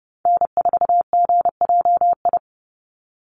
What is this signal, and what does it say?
Morse code, heard as a keyed tone, about 30 wpm, 700 Hz. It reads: N4GJI
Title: USA Call Signs 30wpm